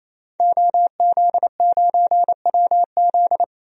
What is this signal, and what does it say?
Morse code, heard as a keyed tone, about 27 wpm, 700 Hz. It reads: OZ9WZ
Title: DXCC-100 Call Signs - 5 Characters - 28wpm